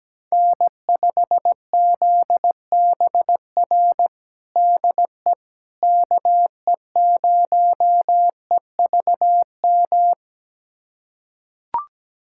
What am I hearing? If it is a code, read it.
Morse code, heard as a keyed tone, about 17 wpm, 700 Hz. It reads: N5ZBR DE KE0EVM
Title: USA Callsign DE Callsign 17wpm